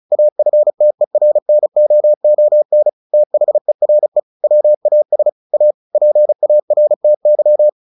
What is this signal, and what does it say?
Morse code, heard as a keyed tone, about 35 wpm, 600 Hz. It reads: AFTERNOON THERE WAS A PARTY
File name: the old man and the sea 2 35wpm